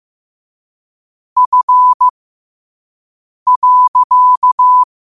Morse code, heard as a keyed tone, about 15 wpm, spaced as for 5 wpm, 1000 Hz. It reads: F.
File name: qotdfiles / 5wpm